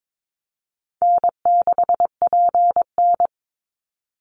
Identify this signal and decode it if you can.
Morse code, heard as a keyed tone, about 22 wpm, 700 Hz. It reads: N6PN
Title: Easy Call Signs Repeated 3x - 22wpm